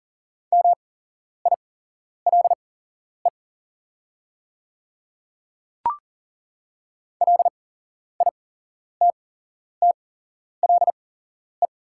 Morse code, heard as a keyed tone, about 40 wpm, spaced as for 11 wpm, 700 Hz. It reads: MILE LITTLE
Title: 8x Character Spacing - Top 300 Words - 40wpm